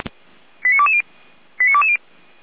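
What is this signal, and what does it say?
Five-tone selective call (ZVEI), heard as quick runs of five tones, tones of about 70 ms. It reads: 89220 89220
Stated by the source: Five-tone ZVEI selective call received on an SDR, from a fire brigade paging channel.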